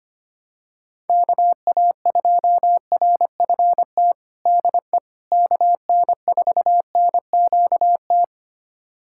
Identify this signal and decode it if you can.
Morse code, heard as a keyed tone, about 25 wpm, 700 Hz. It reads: KA2RFT DE KN4NQT
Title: USA Callsign DE Callsign 25wpm